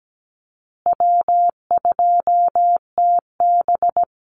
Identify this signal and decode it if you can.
Morse code, heard as a keyed tone, about 17 wpm, 700 Hz. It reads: W2TB
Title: Call Signs Repeated 2x - 17wpm